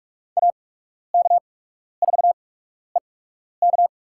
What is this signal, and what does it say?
Morse code, heard as a keyed tone, about 41 wpm, spaced as for 12 wpm, 700 Hz. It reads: AK4EK
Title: USA Call Signs - 5 Characters - 8x Character Spacing - 45wpm